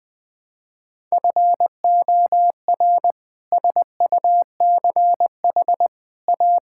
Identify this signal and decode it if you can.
Morse code, heard as a keyed tone, about 19 wpm, 700 Hz. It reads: FOR SUCH A
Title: Sentences from Top 1000 Words - Review Part 6 - 20wpm